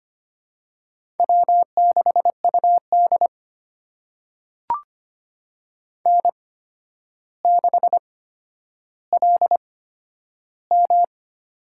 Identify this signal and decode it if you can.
Morse code, heard as a keyed tone, about 25 wpm, spaced as for 11 wpm, 700 Hz. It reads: W6UD N6LM
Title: USA Call Signs - 4 Characters - 8x Character Spacing - 25wpm